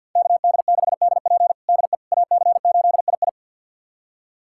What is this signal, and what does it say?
Morse code, heard as a keyed tone, about 52 wpm, 700 Hz. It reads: KD6DP DE AC8II
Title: USA Callsign DE Callsign - 5 Characters - 50wpm